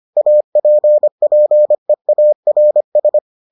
Morse code, heard as a keyed tone, about 25 wpm, 600 Hz. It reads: APPEARS